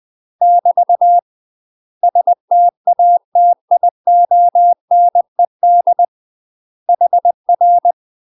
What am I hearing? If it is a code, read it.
Morse code, heard as a keyed tone, about 20 wpm, 700 Hz. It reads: <BT> STATIONED HR
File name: UK MASTER QSOs TRAINING / 20WPM / UK MASTER QSOs 0012